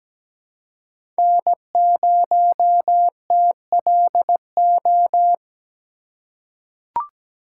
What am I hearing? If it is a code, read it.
Morse code, heard as a keyed tone, about 17 wpm, 700 Hz. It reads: N0TLO
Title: USA Call Signs - 5 Characters - 4x Character Spacing - 17wpm